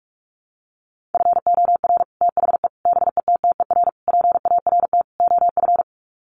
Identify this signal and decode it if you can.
Morse code, heard as a keyed tone, about 44 wpm, 700 Hz. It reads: FOR THE BETTER PART OF
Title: Sets of 5 Words Encore2 45wpm